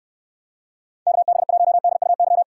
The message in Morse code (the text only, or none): KB1DUY